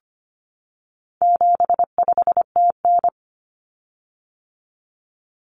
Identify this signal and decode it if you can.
Morse code, heard as a keyed tone, about 25 wpm, 700 Hz. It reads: 75TN
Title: Common Cut Numbers - 4-Digit 25wpm